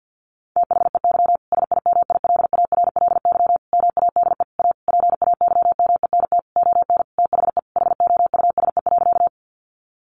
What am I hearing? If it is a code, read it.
Morse code, heard as a keyed tone, about 50 wpm, 700 Hz. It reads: THEY SIMILARLY MADE A PAYMENT ON THE HOUSE.
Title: Sentences from Top 1400 Words 50wpm